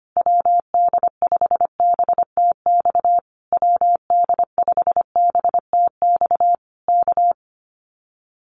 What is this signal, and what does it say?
Morse code, heard as a keyed tone, about 25 wpm, 700 Hz. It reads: WD5BTX WD5BTX K
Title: Common QSO Phrases Encore 25wpm